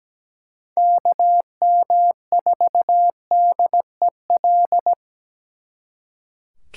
Morse code, heard as a keyed tone, about 20 wpm, 700 Hz. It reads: KM4DEL
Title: Speed Racing - USA 2x3 Call Signs 17wpm